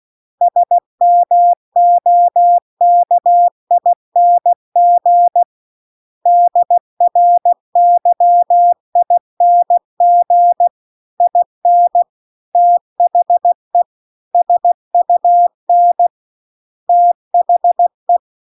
Morse code, heard as a keyed tone, about 16 wpm, 700 Hz. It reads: SMOKING DRYING IN THE SUN THE